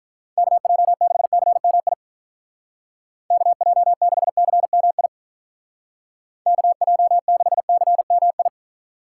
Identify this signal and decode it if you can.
Morse code, heard as a keyed tone, about 43 wpm, 700 Hz. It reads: KJ6CMI KJ6CMI KJ6CMI
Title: Speed Racing - USA Call Signs 35wpm